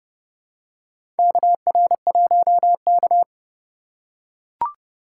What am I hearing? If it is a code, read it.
Morse code, heard as a keyed tone, about 30 wpm, 700 Hz. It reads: KR1K
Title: USA Call Signs - 4 Characters - 3x Character Spacing - 30wpm